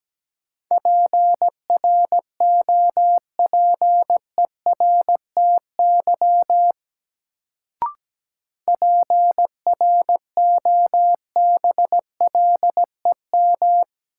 Morse code, heard as a keyed tone, about 17 wpm, 700 Hz. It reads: PROPERTY PROBLEM
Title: Sentences from Top 1000 Words - New Word Drill - 17wpm